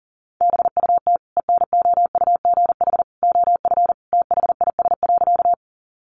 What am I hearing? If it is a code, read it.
Morse code, heard as a keyed tone, about 40 wpm, 700 Hz. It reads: BUT ENOUGH OF THIS.
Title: Sentences from Top 400 Words 40wpm